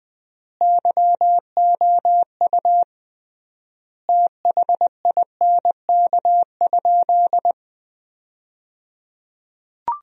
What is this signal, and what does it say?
Morse code, heard as a keyed tone, about 20 wpm, 700 Hz. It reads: YOU THINK?
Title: 3x Word-Spacing - Sentences from Top 200 Words - 20wpm